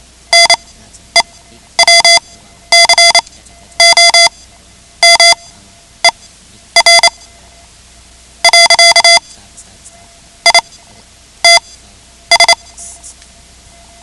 Morse code, heard as a keyed tone, about 28 wpm, spaced as for 11 wpm, 2050 Hz. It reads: NEWCOMER . ITS
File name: ham radio / CodeQuick / Code Quick 10